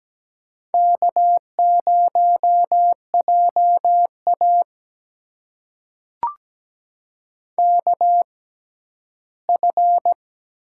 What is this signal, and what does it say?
Morse code, heard as a keyed tone, about 17 wpm, spaced as for 12 wpm, 700 Hz. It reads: K0JA KF
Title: USA Call Signs - 6x Character Spacing - 17wpm